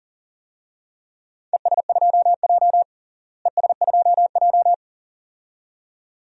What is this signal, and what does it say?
Morse code, heard as a keyed tone, about 40 wpm, 700 Hz. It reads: ES2J ES2J
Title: DXCC-100 Call Signs - 4 Characters - Repeated 2x - 40wpm